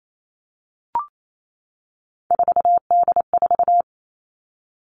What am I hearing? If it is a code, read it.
Morse code, heard as a keyed tone, about 28 wpm, 700 Hz. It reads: 4D4
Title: All Cut Numbers - 3-Digit 28wpm